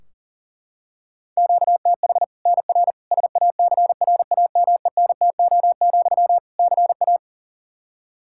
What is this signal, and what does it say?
Morse code, heard as a keyed tone, about 40 wpm, 700 Hz. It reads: QTH NR SACRAMENTO, CA
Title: Practice QSO - Exchange 1 - Beginning - 40wpm